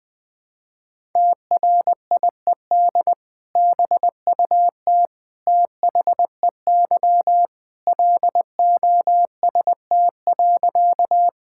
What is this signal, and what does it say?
Morse code, heard as a keyed tone, about 20 wpm, 700 Hz. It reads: TRIED BUT THEY LOST.
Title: Sentences with 2-Word Phrases - Spoken Beginning - 20wpm